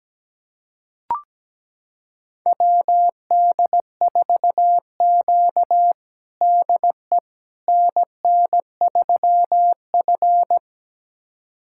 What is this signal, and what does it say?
Morse code, heard as a keyed tone, about 17 wpm, 700 Hz. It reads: WD4Q DE NN3F
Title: USA Callsign DE Callsign - 4 Characters - 17wpm